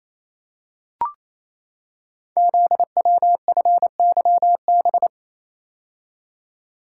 Morse code, heard as a keyed tone, about 28 wpm, 700 Hz. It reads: ZWFYB
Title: Code Groups - 5 Letters - 28wpm